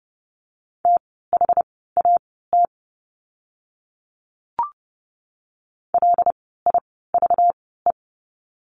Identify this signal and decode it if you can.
Morse code, heard as a keyed tone, about 30 wpm, spaced as for 17 wpm, 700 Hz. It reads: THAT LIVE
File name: course / 30:10wpm / 237 Lesson 38f - Letter q - Words 30wpm